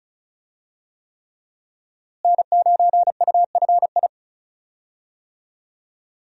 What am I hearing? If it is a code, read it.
Morse code, heard as a keyed tone, about 36 wpm, 700 Hz. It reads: N9UFI
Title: USA Call Signs - 5 Characters - Repeated 3x - 35wpm